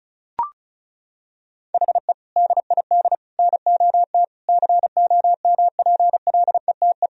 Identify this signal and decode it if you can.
Morse code, heard as a keyed tone, about 36 wpm, 700 Hz. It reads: HE DID NOT COMPLETE
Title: Sentences from Top 1000 Words - Review Part 1 - 35wpm